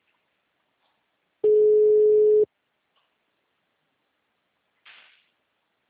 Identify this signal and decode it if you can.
Ringback tone (Europe, 425 Hz).
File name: out- -2005- - - .23017